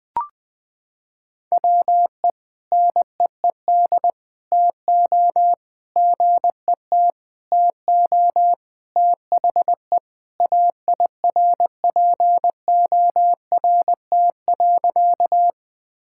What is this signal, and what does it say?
Morse code, heard as a keyed tone, about 21 wpm, 700 Hz. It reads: WE NEED TO GET TO THE AIRPORT.
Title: Sentences with 3-Word Phrases - 20wpm